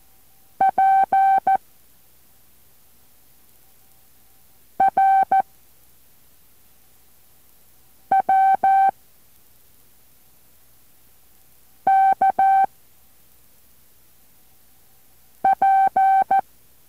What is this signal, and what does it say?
Morse code, heard as a keyed tone, about 14 wpm, 770 Hz. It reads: P R W K P